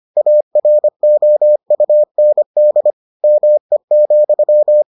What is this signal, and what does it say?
Morse code, heard as a keyed tone, about 25 wpm, 600 Hz. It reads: AROUND ME,